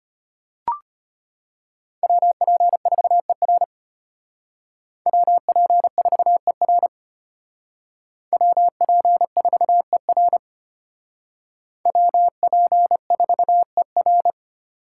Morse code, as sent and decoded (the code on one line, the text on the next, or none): .-- .--. ....- . .-. / .-- .--. ....- . .-. / .-- .--. ....- . .-. / .-- .--. ....- . .-.
WP4ER WP4ER WP4ER WP4ER